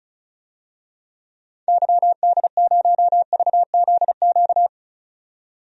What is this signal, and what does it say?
Morse code, heard as a keyed tone, about 36 wpm, 700 Hz. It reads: YD0VZQ
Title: DXCC-100 Call Signs - 6 Characters - 35wpm